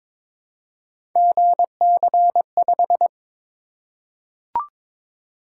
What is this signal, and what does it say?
Morse code, heard as a keyed tone, about 22 wpm, 700 Hz. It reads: GC5
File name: practice / 22wpm / Three Letter-Number Combinations - 22wpm